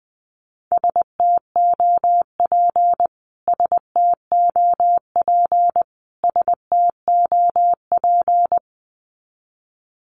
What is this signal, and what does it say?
Morse code, heard as a keyed tone, about 20 wpm, 700 Hz. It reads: STOP STOP STOP
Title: Top 500 Words Repeated 20wpm